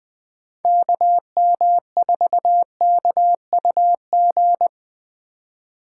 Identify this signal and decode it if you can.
Morse code, heard as a keyed tone, about 21 wpm, 700 Hz. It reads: KM4KUG
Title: Speed Racing - USA 2x3 Call Signs 17wpm